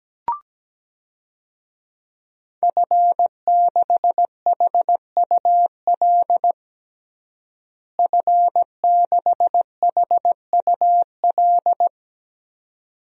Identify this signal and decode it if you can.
Morse code, heard as a keyed tone, about 17 wpm, 700 Hz. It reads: F6HUL F6HUL
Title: DXCC-100 Call Signs - 5 Characters - Repeated 2x - 17wpm